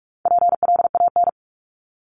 Morse code, heard as a keyed tone, about 44 wpm, 700 Hz. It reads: PLAN
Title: ING Suffix - Top 100 Words with Base Words 45wpm